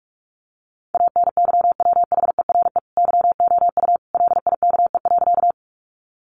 Morse code, heard as a keyed tone, about 44 wpm, 700 Hz. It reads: ANYWHERE YOU LIKE.
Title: Sentences from Top 1400 Words 45wpm